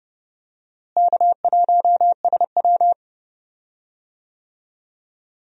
Morse code, heard as a keyed tone, about 30 wpm, 700 Hz. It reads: K1SW
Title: USA Call Signs - 4 Characters - Repeated 3x - 30wpm